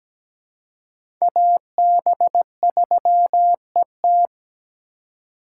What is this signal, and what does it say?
Morse code, heard as a keyed tone, about 17 wpm, 700 Hz. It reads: AB3ET
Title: USA Call Signs - 5 Characters - 3x Character Spacing - 17wpm